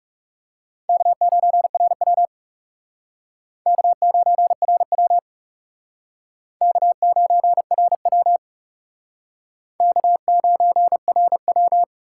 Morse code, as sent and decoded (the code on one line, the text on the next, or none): -.- ----. .-. .-- / -.- ----. .-. .-- / -.- ----. .-. .-- / -.- ----. .-. .--
K9RW K9RW K9RW K9RW